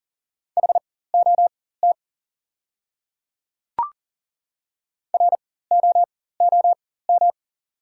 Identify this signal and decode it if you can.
Morse code, heard as a keyed tone, about 40 wpm, spaced as for 19 wpm, 690 Hz. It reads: HOT ROOM
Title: 4x Character Spacing - Top 300 Words - 40wpm